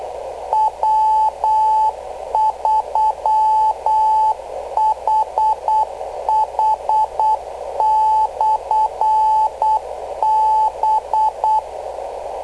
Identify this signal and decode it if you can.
Morse code, heard as a keyed tone, about 8 wpm, 870 Hz, with radio noise. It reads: W3HH/B
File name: kl7fdq / beacons / w3hh-rst589